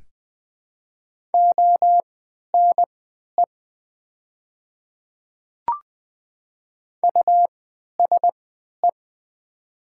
Morse code, heard as a keyed tone, about 20 wpm, spaced as for 11 wpm, 700 Hz. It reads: ONE USE
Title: 3x Character Spacing - Top 100 Words - 20wpm